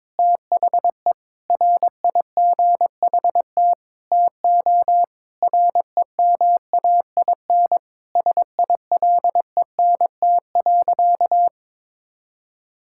Morse code, heard as a keyed tone, about 22 wpm, 700 Hz. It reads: THE RIGHT TO REMAIN SILENT.
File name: practice / 22wpm / Sentences with 4-Word Phrases - 22wpm